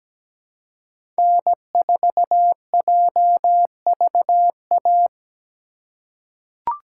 Morse code, heard as a keyed tone, about 17 wpm, 700 Hz. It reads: N4JVA
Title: USA Call Signs - 3x Character Spacing - 17wpm